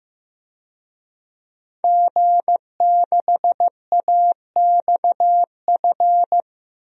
Morse code, heard as a keyed tone, about 15 wpm, 700 Hz. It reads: G6AXF
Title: DXCC-100 Call Signs - 5 Characters - 15wpm